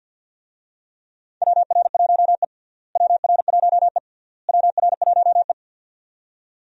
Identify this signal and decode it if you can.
Morse code, heard as a keyed tone, about 47 wpm, 700 Hz. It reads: WR1E WR1E WR1E
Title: USA Call Signs - 4 Characters - Repeated 3x - 50wpm